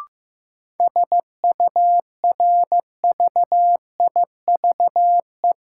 Morse code, heard as a keyed tone, about 15 wpm, 700 Hz. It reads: SURVIVE
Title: Sentences from Top 1300 Words - New Word Drill Alt - 15wpm